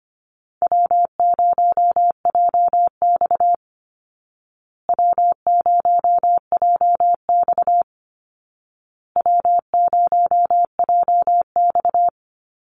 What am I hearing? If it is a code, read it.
Morse code, heard as a keyed tone, about 25 wpm, 700 Hz. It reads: W0JX W0JX W0JX
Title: Easy Call Signs Repeated 3x - 25wpm